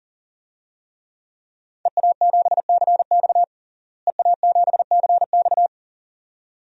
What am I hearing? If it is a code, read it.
Morse code, heard as a keyed tone, about 37 wpm, 700 Hz. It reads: EA7CX EA7CX
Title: DXCC-100 Call Signs - 5 Characters - Repeated 2x - 40wpm